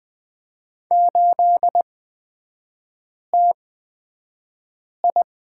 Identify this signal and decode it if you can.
Morse code, heard as a keyed tone, about 20 wpm, spaced as for 5 wpm, 700 Hz. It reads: 8TI
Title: 250 Lesson 40e - BK - Three Characters 20wpm